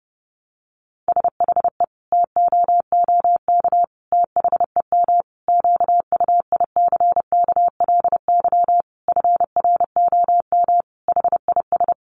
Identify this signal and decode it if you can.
Morse code, heard as a keyed tone, about 30 wpm, 700 Hz. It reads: SHE TOOK THEM QUICKLY FROM HIS
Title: Sentences from Top 1300 Words 30wpm